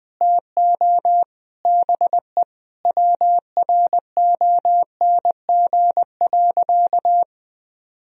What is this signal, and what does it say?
Morse code, heard as a keyed tone, about 20 wpm, 700 Hz. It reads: TO BE WRONG.
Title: Sentences from Top 1000 Words - Review Part 3 - 20wpm